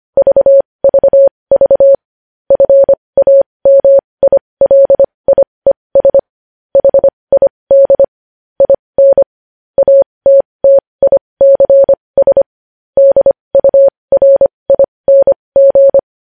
Morse code, heard as a keyed tone, about 25 wpm, spaced as for 20 wpm, 550 Hz. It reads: VVV FAMILIES HID IN ATTICS DURING
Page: News headlines in Morse code at 20 wpm